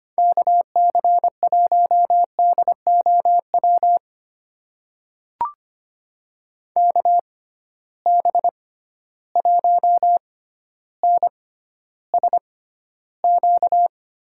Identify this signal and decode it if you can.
Morse code, heard as a keyed tone, about 25 wpm, spaced as for 13 wpm, 700 Hz. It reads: KC1DOW KB1NSQ
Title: USA Call Signs - 6 Characters - 6x Character Spacing - 25wpm